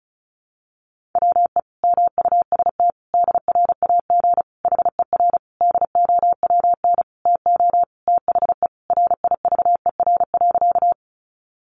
Morse code, heard as a keyed tone, about 34 wpm, 700 Hz. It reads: WE MUST DRAG HER DOWN TO THE RIVER.
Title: Sentences from Top 1500 Words 35wpm